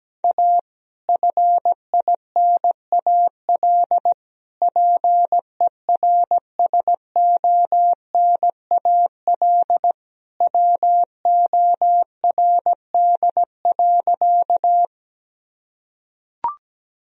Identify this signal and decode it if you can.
Morse code, heard as a keyed tone, about 17 wpm, 700 Hz. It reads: A FINAL PERSONAL WORD.
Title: Sentences from Top 1000 Words - Review Part 4 - 17wpm